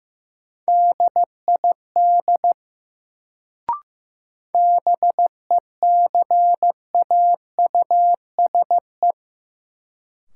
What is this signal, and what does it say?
Morse code, heard as a keyed tone, about 15 wpm, 700 Hz. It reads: DID BECAUSE
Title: Sets of 2 Words - New Word Drill - 15wpm